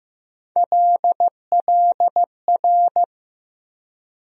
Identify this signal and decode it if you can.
Morse code, heard as a keyed tone, about 15 wpm, 700 Hz. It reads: LLR
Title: USA Call Sign Postfixes - 3 Letters 15wpm